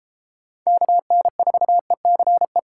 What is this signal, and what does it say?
Morse code, heard as a keyed tone, about 33 wpm, 700 Hz. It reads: KN4ECE